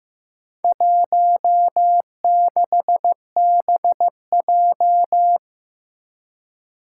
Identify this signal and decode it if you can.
Morse code, heard as a keyed tone, about 15 wpm, 700 Hz. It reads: 16BJ
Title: Four Challenging Letter-Number Combinations 15wpm